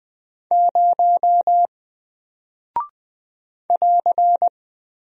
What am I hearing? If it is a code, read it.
Morse code, heard as a keyed tone, about 20 wpm, 700 Hz. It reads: 0 <AR>
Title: Single Character (Letter:Number:Punctuation:Prosign) Alt 20wpm